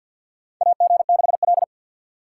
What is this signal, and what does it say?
Morse code, heard as a keyed tone, about 52 wpm, 700 Hz. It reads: AG6L
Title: USA Call Signs - 3x Character Spacing - 50wpm